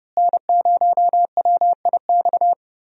Morse code, heard as a keyed tone, about 32 wpm, 700 Hz. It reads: N0WIX
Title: USA Call Signs - 5 Characters - Repeated 3x - 30wpm